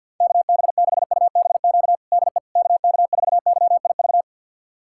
Morse code, heard as a keyed tone, about 53 wpm, 700 Hz. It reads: KB6UBQ DE KK4YIV
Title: USA Callsign DE Callsign - 6 Characters - 50wpm